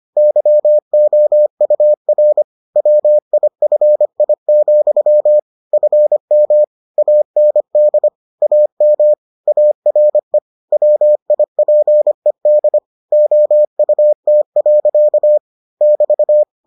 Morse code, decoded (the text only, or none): YOUR WIFI, FM AND AM ARE WIPED OUT. <BT>